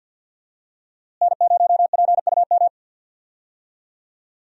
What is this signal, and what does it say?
Morse code, heard as a keyed tone, about 47 wpm, 700 Hz. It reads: N0PUM
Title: USA Call Signs - 5 Characters - Repeated 2x - 50wpm